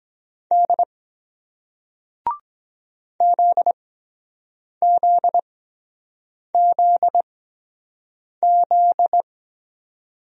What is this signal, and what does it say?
Morse code, heard as a keyed tone, about 22 wpm, spaced as for 7 wpm, 700 Hz. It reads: D ZZZZ